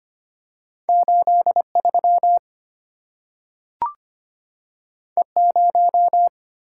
Morse code, heard as a keyed tone, about 25 wpm, 700 Hz. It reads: 83 E0
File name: Two Letter-Number Combinations - 25wpm